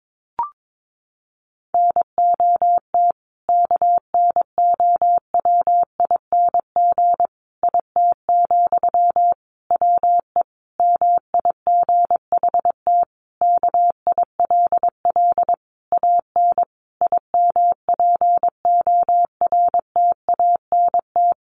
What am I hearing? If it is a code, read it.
Morse code, heard as a keyed tone, about 22 wpm, 700 Hz. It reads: NOT KNOWING IT, WE MIGHT KILL AN IMPORTANT
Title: A Sound Of Thunder 22wpm